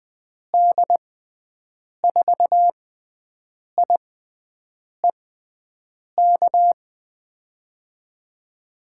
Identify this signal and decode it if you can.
Morse code, heard as a keyed tone, about 20 wpm, spaced as for 7 wpm, 700 Hz. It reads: D4IEK
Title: USA Call Signs - 6 Characters - 6x Character Spacing - 20wpm